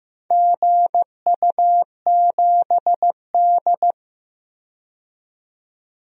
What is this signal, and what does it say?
Morse code, heard as a keyed tone, about 15 wpm, 700 Hz. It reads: GU7D
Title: DXCC-100 Call Signs - 4 Characters - 15wpm